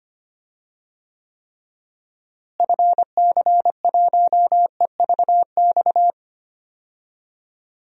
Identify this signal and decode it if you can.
Morse code, heard as a keyed tone, about 25 wpm, 700 Hz. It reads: FC1EVX
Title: DXCC-100 Call Signs - 6 Characters - Repeated 3x - 25wpm